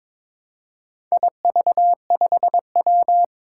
Morse code, heard as a keyed tone, about 22 wpm, 700 Hz. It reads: IV5W
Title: Four Challenging Letter-Number Combinations 22wpm